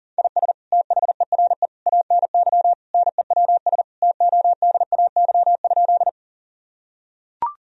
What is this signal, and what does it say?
Morse code, heard as a keyed tone, about 40 wpm, 700 Hz. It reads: IS THERE ANY NEWS TODAY?
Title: Sentences from Top 1000 Words - Review Part 4 - 40wpm